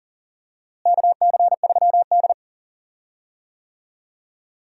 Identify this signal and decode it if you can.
Morse code, heard as a keyed tone, about 40 wpm, 700 Hz. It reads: KC3D
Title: USA Call Signs - 4 Characters - Repeated 3x - 40wpm